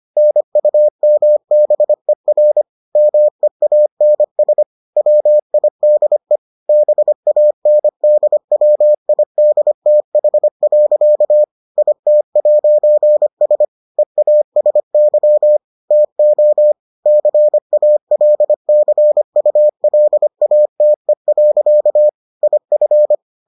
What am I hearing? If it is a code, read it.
Morse code, heard as a keyed tone, about 25 wpm, 600 Hz. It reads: NUMBER MEANS WIDE BANDWIDTH. IT'S EASY TO CALCULATE. IF